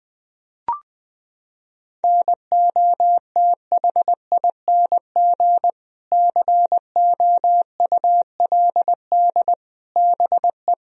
Morse code, heard as a keyed tone, about 20 wpm, 700 Hz. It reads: NOTHING COULD BE
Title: Sentences from Top 1000 Words - Review Part 4 - 20wpm